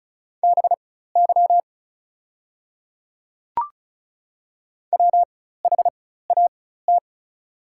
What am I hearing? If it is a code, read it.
Morse code, heard as a keyed tone, about 34 wpm, spaced as for 16 wpm, 700 Hz. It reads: BY WHAT
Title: 4x Character Spacing - Top 100 Words - 35wpm